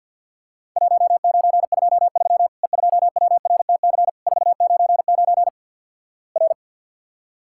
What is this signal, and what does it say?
Morse code, heard as a keyed tone, about 50 wpm, 700 Hz. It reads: 1923 E2WRTC 499 R
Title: Worked All Europe DX Contest - QTC Exchange with Repeats - 50wpm